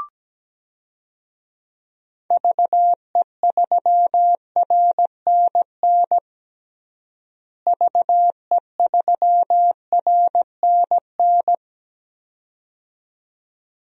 Morse code, heard as a keyed tone, about 17 wpm, 700 Hz. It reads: VE3RNN VE3RNN
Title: DXCC-100 Call Signs - 6 Characters - Repeated 2x - 17wpm